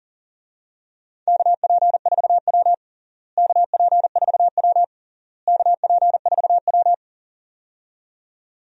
Morse code, heard as a keyed tone, about 40 wpm, 700 Hz. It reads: KP4W KP4W KP4W
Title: DXCC-100 Call Signs - 4 Characters - Repeated 3x - 40wpm